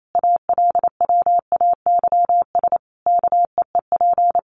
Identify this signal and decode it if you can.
Morse code, heard as a keyed tone, about 28 wpm, 700 Hz. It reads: ALWAYS KEEP
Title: Sentences from Top 1300 Words 28wpm